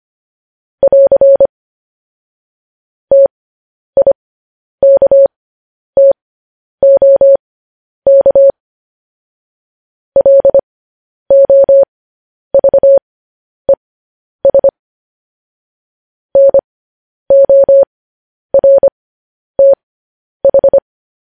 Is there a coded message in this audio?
Yes: Morse code from 0.8 to 20.8 s, heard as a keyed tone, about 25 wpm, spaced as for 10 wpm, 550 Hz. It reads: <AR> TIKTOK LOVES NORTH